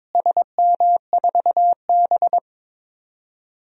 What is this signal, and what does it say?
Morse code, heard as a keyed tone, about 22 wpm, 700 Hz. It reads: SM4B
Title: DXCC-100 Call Signs - 4 Characters - Repeated 3x - 22wpm